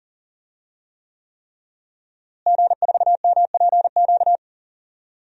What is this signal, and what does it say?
Morse code, heard as a keyed tone, about 38 wpm, 700 Hz. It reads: G4MPQ
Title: DXCC-100 Call Signs - 5 Characters - Repeated 3x - 40wpm